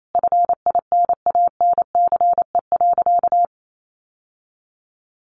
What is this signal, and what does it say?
Morse code, heard as a keyed tone, about 28 wpm, 700 Hz. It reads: FINANCE.
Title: Sentences from Top 1500 Words 28wpm